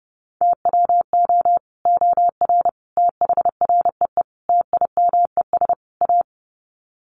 Morse code, heard as a keyed tone, about 30 wpm, 700 Hz. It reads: TWO OR THREE TIMES A
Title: Sets of 5 Words Encore2 30wpm